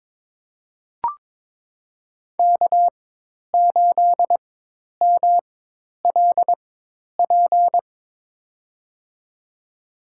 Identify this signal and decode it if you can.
Morse code, heard as a keyed tone, about 22 wpm, spaced as for 10 wpm, 700 Hz. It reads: K8MLP
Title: USA Call Signs - 5 Characters - 4x Character Spacing - 22wpm